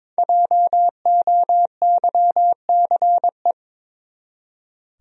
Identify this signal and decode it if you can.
Morse code, heard as a keyed tone, about 22 wpm, 700 Hz. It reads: JOYCE
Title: USA Common Names - Max 5 Letters - 22wpm